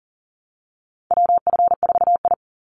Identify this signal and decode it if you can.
Morse code, heard as a keyed tone, about 40 wpm, 700 Hz. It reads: WF4I
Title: NA SOTA Chaser Callsigns 40wpm